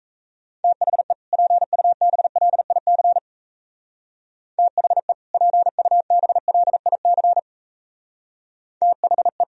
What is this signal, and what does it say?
Morse code, heard as a keyed tone, about 37 wpm, 700 Hz. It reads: THE PUBLIC THE PUBLIC THE